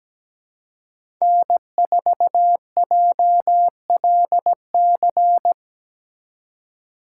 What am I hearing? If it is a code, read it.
Morse code, heard as a keyed tone, about 17 wpm, 700 Hz. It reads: N4JLC
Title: USA Call Signs - 5 Characters - 2x Character Spacing - 17wpm